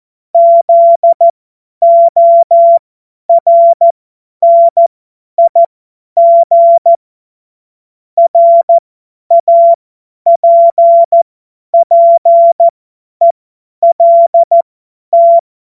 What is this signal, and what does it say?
Morse code, heard as a keyed tone, about 14 wpm, spaced as for 10 wpm, 680 Hz. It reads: ZORNIG RAPPELT
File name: Morsekurs / Witze / Witz 14. Februar 2023 50 F70